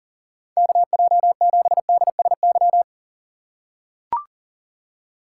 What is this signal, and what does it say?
Morse code, heard as a keyed tone, about 40 wpm, 700 Hz. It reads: KJ7DSY
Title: USA Call Signs - 6x Character Spacing - 40wpm